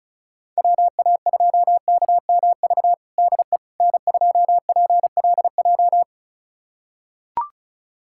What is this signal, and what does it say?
Morse code, heard as a keyed tone, about 36 wpm, 700 Hz. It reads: WA2KMV DE N2PLJ
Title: USA Callsign DE Callsign 35wpm